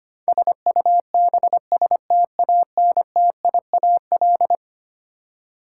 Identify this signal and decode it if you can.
Morse code, heard as a keyed tone, about 25 wpm, 700 Hz. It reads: SUBSTANTIAL
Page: morse-code-ninja-practice-25wpm